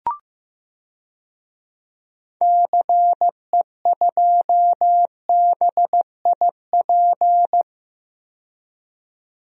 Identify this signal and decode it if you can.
Morse code, heard as a keyed tone, about 15 wpm, 700 Hz. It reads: CE2BIP
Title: DXCC-100 Call Signs - 6 Characters - 15wpm